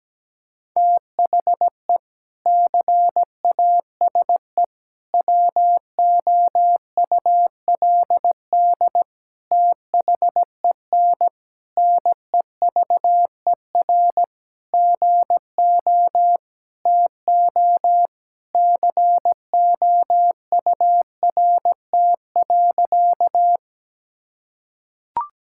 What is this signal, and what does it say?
Morse code, heard as a keyed tone, about 17 wpm, 700 Hz. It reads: THE CASE WOULD THEN NEVER GO TO COURT.
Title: Sentences from Top 1600 Words 17wpm